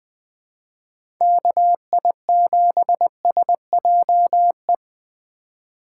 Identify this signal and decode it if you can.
Morse code, heard as a keyed tone, about 20 wpm, 700 Hz. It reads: KI7SJE
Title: USA Call Signs - 6 Characters - 3x Character Spacing - 20wpm